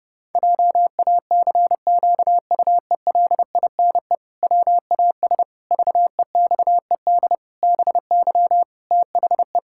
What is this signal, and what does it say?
Morse code, heard as a keyed tone, about 32 wpm, 700 Hz. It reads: JACQUELINE WAS VEXED BY THE
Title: Pangram Challenge v2 - 30wpm